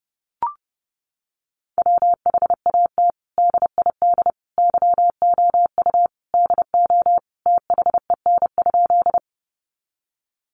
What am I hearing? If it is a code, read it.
Morse code, heard as a keyed tone, about 30 wpm, 700 Hz. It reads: WHAT DID YOU DO THEN?
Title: Sentences from Top 100 Words 30wpm